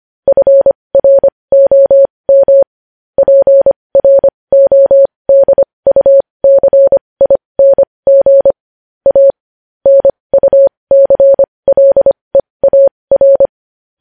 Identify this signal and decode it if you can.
Morse code, heard as a keyed tone, about 25 wpm, spaced as for 20 wpm, 550 Hz. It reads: FROM PRODUCING A NUCLEAR
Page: News headlines in Morse code at 20 WPM